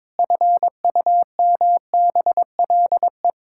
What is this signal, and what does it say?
Morse code, heard as a keyed tone, about 21 wpm, 700 Hz. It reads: FUMBLE
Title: Pangram Challenge v2 - 22wpm